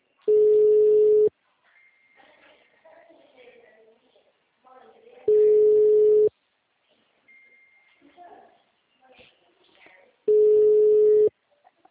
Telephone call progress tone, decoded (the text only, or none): ringback tone